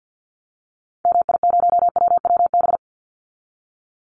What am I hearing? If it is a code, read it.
Morse code, heard as a keyed tone, about 50 wpm, 700 Hz. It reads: MI0WWB
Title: Call Signs - 50wpm